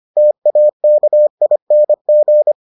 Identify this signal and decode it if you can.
Morse code, heard as a keyed tone, about 25 wpm, 600 Hz. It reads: TAKING